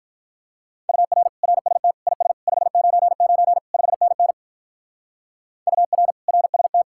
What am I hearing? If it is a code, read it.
Morse code, heard as a keyed tone, about 44 wpm, 700 Hz. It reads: UR RST IS 599 5NN UR RST